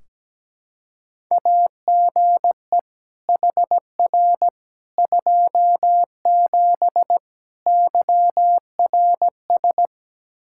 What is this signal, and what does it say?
Morse code, heard as a keyed tone, about 17 wpm, 700 Hz. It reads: AGE HR 27 YRS
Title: Practice QSO - Exchange 3 - More Info - 17wpm